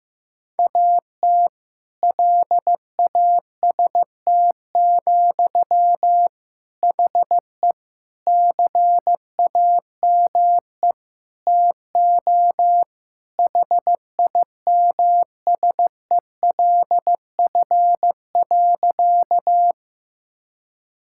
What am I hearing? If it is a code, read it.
Morse code, heard as a keyed tone, about 15 wpm, 700 Hz. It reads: AT LAST, HE CAME TO HIMSELF.
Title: Sentences from Top 1600 Words 15wpm